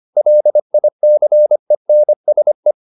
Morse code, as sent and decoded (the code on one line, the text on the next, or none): .-.. .. -.-. -. ... .
LICENSE